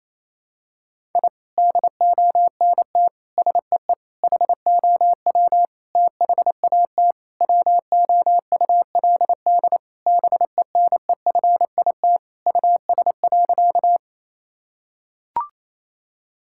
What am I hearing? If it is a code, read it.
Morse code, heard as a keyed tone, about 27 wpm, 700 Hz. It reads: I DONT SEE HOW THAT WOULD BENEFIT US.
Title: Sentences from Top 1000 Words - Review Part 6 - 28wpm